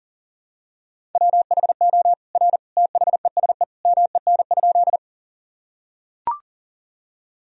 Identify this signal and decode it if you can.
Morse code, heard as a keyed tone, about 37 wpm, 700 Hz. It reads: WHO R THESE MEN?
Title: Sentences from Top 200 Words and Abbreviations 40wpm